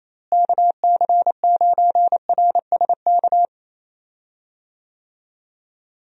Morse code, heard as a keyed tone, about 28 wpm, 700 Hz. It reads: KC9RSK
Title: USA Call Signs - 6 Characters - Repeated 2x - 28wpm